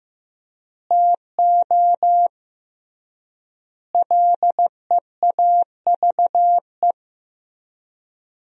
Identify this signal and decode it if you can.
Morse code, heard as a keyed tone, about 15 wpm, 700 Hz. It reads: TO LEAVE
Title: 3x Word-Spacing - Sentences from Top 400 Words - 15wpm